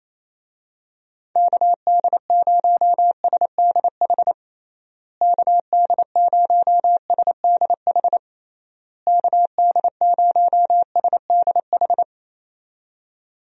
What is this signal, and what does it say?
Morse code, heard as a keyed tone, about 28 wpm, 700 Hz. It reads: KD0SDH KD0SDH KD0SDH
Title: USA Call Signs - 6 Characters - Repeated 3x - 28wpm